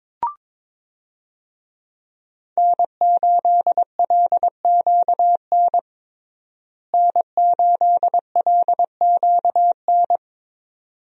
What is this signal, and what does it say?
Morse code, heard as a keyed tone, about 21 wpm, 700 Hz. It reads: N8LQN N8LQN
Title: USA Call Signs - 5 Characters - Repeated 2x - 22wpm